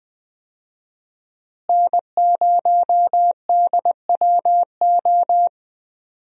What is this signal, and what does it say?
Morse code, heard as a keyed tone, about 21 wpm, 700 Hz. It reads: N0DWO
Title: USA Call Signs - 5 Characters - Repeated 2x - 20wpm